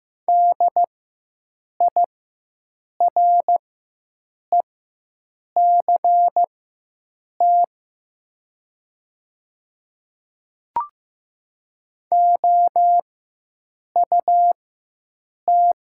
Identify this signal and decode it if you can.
Morse code, heard as a keyed tone, about 15 wpm, spaced as for 7 wpm, 700 Hz. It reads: DIRECT OUT
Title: 4x Character Spacing - Top 500 Words - 15wpm